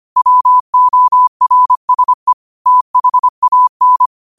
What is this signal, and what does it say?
Morse code, heard as a keyed tone, about 25 wpm, 1000 Hz. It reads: WORSE THAN